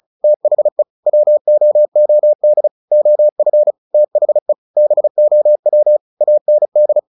Morse code, heard as a keyed tone, about 34 wpm, 600 Hz. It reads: THE WOOD OF THE BOW AND
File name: the old man and the sea / the old man and the sea 1 35wpm